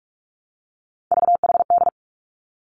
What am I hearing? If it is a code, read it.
Morse code, heard as a keyed tone, about 46 wpm, 700 Hz. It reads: VHD
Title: Three Challenging Letter-Number Combinations 45wpm